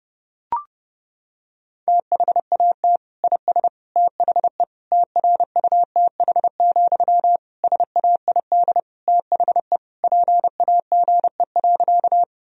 Morse code, heard as a keyed tone, about 32 wpm, 700 Hz. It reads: THAT IS THE TRUTH, SAID THE PAGE.
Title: Sentences from Top 1000 Words - Review Part 1 - 30wpm